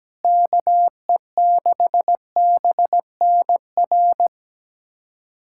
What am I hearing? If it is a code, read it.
Morse code, heard as a keyed tone, about 17 wpm, 700 Hz. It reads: KE6BNR
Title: USA Callsign DE Callsign - 6 Characters - 17wpm